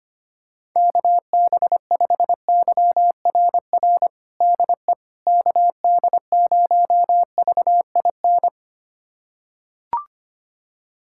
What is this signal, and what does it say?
Morse code, heard as a keyed tone, about 25 wpm, 700 Hz. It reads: KB5YRR DE KD0VIN
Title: USA Callsign DE Callsign 25wpm